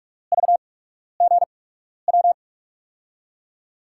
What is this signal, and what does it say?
Morse code, heard as a keyed tone, about 43 wpm, spaced as for 12 wpm, 700 Hz. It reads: VGW
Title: USA Call Signs - 6 Characters - 8x Character Spacing - 45wpm